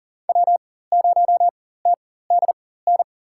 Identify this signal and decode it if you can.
Morse code, heard as a keyed tone, about 40 wpm, spaced as for 19 wpm, 700 Hz. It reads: W0TDN
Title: USA Call Signs - 5 Characters - 4x Character Spacing - 40wpm